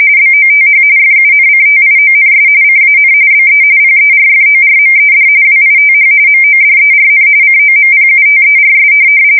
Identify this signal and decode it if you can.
RTTY, about 45.45 baud. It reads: BOTH FUNCTIONS CAN BE PERFORMED WITH MODERN COMPUTERS
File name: fsk170 / teletype-RTTY